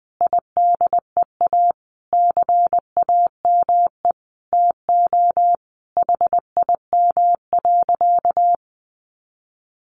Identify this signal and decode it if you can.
Morse code, heard as a keyed tone, about 19 wpm, 700 Hz. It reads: IDEA CAME TO HIM.
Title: Sentences from Top 1000 Words 20wpm